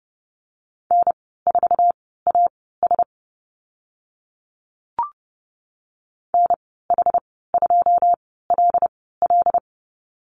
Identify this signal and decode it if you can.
Morse code, heard as a keyed tone, about 30 wpm, spaced as for 17 wpm, 700 Hz. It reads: N4AS NH2LL
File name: Lesson 14g - Number 2 - Call Signs 30wpm